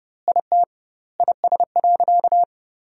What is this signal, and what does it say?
Morse code, heard as a keyed tone, about 32 wpm, 700 Hz. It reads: IT IS.
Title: 2x Word-Spacing - Sentences from Top 200 Words - 30wpm